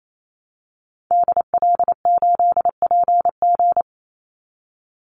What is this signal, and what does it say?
Morse code, heard as a keyed tone, about 28 wpm, 700 Hz. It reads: DL8PG
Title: Call Signs - 28wpm